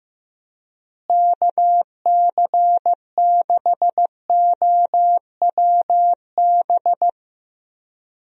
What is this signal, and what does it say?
Morse code, heard as a keyed tone, about 15 wpm, 700 Hz. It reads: KC6OWB
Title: USA Call Signs - 6 Characters - Repeated 3x - 15wpm